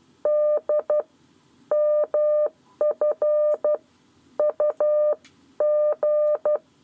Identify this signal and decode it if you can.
Morse code, heard as a keyed tone, about 12 wpm, 600 Hz. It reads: D MF UG
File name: 7 Lezione cw 1